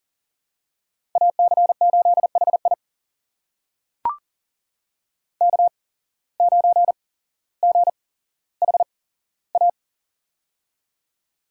Morse code, heard as a keyed tone, about 40 wpm, spaced as for 17 wpm, 700 Hz. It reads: AC8HI K9GHA
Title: USA Call Signs - 5 Characters - 8x Character Spacing - 40wpm